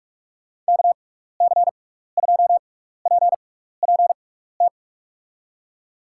Morse code, heard as a keyed tone, about 44 wpm, spaced as for 15 wpm, 700 Hz. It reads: KC2PPT
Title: USA Call Signs - 6 Characters - 6x Character Spacing - 45wpm